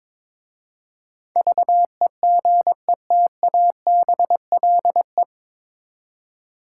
Morse code, heard as a keyed tone, about 22 wpm, 700 Hz. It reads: VEGETABLE
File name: 9 Letter Words - First 3 Letters Given - 22wpm